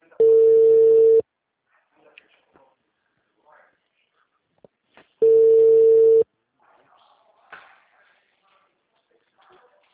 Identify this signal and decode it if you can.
Ringback tone (China, 450 Hz), with speech.